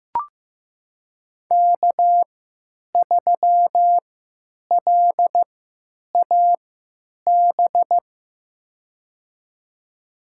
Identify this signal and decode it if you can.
Morse code, heard as a keyed tone, about 15 wpm, spaced as for 9 wpm, 700 Hz. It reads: K3LAB
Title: USA Call Signs - 5 Characters - 3x Character Spacing - 15wpm